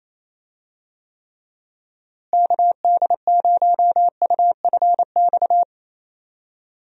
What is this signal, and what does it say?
Morse code, heard as a keyed tone, about 27 wpm, 700 Hz. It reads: KD0UFX
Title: USA Call Signs - 6 Characters - Repeated 2x - 28wpm